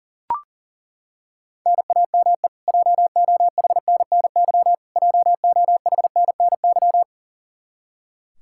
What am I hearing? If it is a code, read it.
Morse code, heard as a keyed tone, about 40 wpm, 700 Hz. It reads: NAME JOHNNY JOHNNY
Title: Practice QSO - Exchange 1 - Beginning - 40wpm